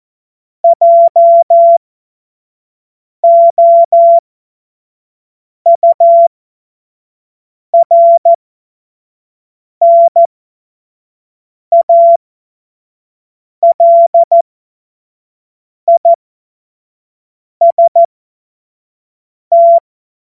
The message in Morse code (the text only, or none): JOURNALIST